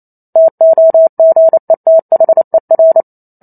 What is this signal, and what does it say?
Morse code, heard as a keyed tone, about 28 wpm, 650 Hz. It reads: TOGETHER